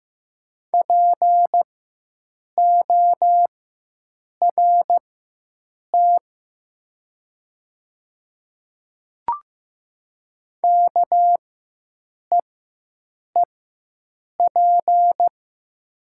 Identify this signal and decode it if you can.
Morse code, heard as a keyed tone, about 15 wpm, spaced as for 7 wpm, 700 Hz. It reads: PORT KEEP
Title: 4x Character Spacing - Top 500 Words - 15wpm